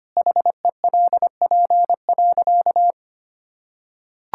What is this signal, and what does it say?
Morse code, heard as a keyed tone, about 25 wpm, 700 Hz. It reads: HELP.
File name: practice / 25wpm / Sentences with 4-Word Phrases - 25wpm